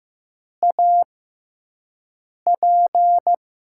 Morse code, heard as a keyed tone, about 15 wpm, spaced as for 5 wpm, 700 Hz. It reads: AP
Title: USA Call Signs - 5 Characters - 6x Character Spacing - 15wpm